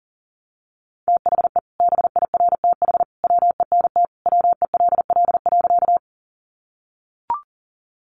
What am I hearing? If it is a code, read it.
Morse code, heard as a keyed tone, about 40 wpm, 700 Hz. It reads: THE BIRTH WENT WELL.
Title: Sentences from Top 800 Words 40wpm